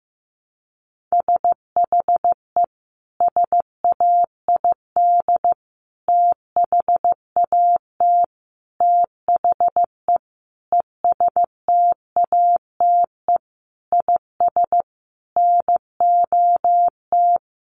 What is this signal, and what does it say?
Morse code, heard as a keyed tone, about 15 wpm, 700 Hz. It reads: SHE SAID THAT THE ESTATE IS NOT
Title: Sentences from Top 1200 Words 15wpm